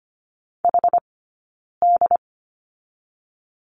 Morse code, heard as a keyed tone, about 25 wpm, 700 Hz. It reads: H D
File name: Lesson 15d - Number 5 - Two Characters 25wpm